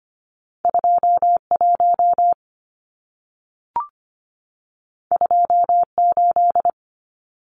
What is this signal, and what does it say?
Morse code, heard as a keyed tone, about 25 wpm, 700 Hz. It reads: 21 28
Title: Numbers - 2-Digit 25wpm